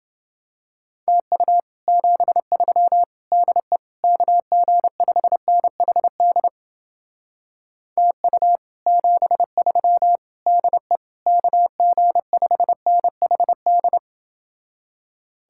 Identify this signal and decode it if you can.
Morse code, heard as a keyed tone, about 30 wpm, 700 Hz. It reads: TU 73 DE KG5NHD TU 73 DE KG5NHD